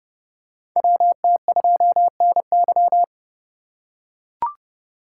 Morse code, heard as a keyed tone, about 30 wpm, 700 Hz. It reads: WT2NY
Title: USA Call Signs - 5 Characters - 4x Character Spacing - 30wpm